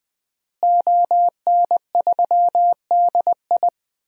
Morse code, heard as a keyed tone, about 20 wpm, 700 Hz. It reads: ON3DI
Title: DXCC-100 Call Signs - 5 Characters - Repeated 3x - 20wpm